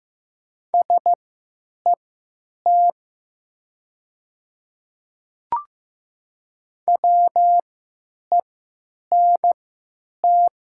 Morse code, heard as a keyed tone, about 15 wpm, spaced as for 9 wpm, 700 Hz. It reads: SET WENT
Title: 3x Character Spacing - Top 300 Words - 15wpm